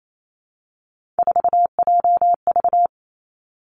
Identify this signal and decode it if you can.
Morse code, heard as a keyed tone, about 28 wpm, 700 Hz. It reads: 4JV
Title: Three Challenging Letter-Number Combinations 28wpm